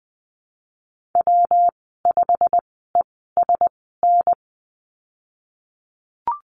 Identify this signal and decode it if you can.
Morse code, heard as a keyed tone, about 20 wpm, spaced as for 15 wpm, 700 Hz. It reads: W5ESN
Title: 147 Lesson 24g - Letter y - Call Signs 20wpm